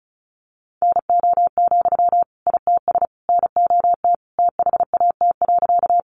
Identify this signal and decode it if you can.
Morse code, heard as a keyed tone, about 34 wpm, 700 Hz. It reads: NO, ITS NOT THAT.
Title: Sentences from Top 600 Words 35wpm